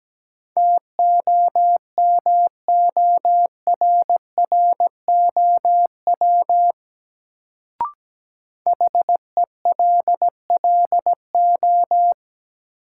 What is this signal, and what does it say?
Morse code, heard as a keyed tone, about 17 wpm, 700 Hz. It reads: TOMORROW HELLO
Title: Sentences from Top 1100 Words - New Word Drill Alt - 17wpm